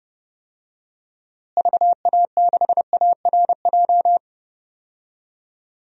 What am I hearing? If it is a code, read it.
Morse code, heard as a keyed tone, about 30 wpm, 700 Hz. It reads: VA6ARJ
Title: DXCC-100 Call Signs - 6 Characters - Repeated 2x - 30wpm